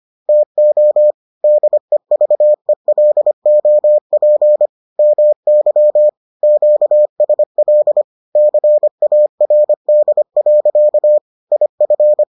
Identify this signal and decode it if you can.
Morse code, heard as a keyed tone, about 25 wpm, 600 Hz. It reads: TO DEVELOP MY QSL CARD. IF